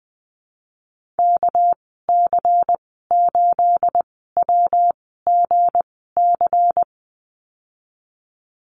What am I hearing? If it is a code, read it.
Morse code, heard as a keyed tone, about 20 wpm, spaced as for 15 wpm, 700 Hz. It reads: KC8WGC
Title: 213 Lesson 34g - Number 8 - Call Signs 20wpm